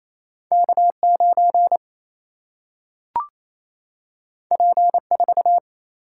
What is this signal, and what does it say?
Morse code, heard as a keyed tone, about 28 wpm, 700 Hz. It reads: K 9 P 4
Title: Single-Letter Single-Number 28wpm